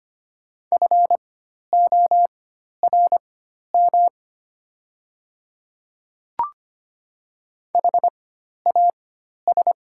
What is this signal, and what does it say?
Morse code, heard as a keyed tone, about 25 wpm, spaced as for 12 wpm, 700 Hz. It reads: FORM HAS
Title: 4x Character Spacing - Top 200 Words - 25wpm